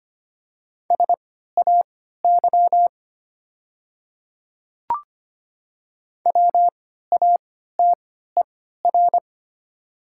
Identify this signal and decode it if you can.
Morse code, heard as a keyed tone, about 25 wpm, spaced as for 14 wpm, 700 Hz. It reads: SAY WATER
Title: 3x Character Spacing - Top 200 Words - 25wpm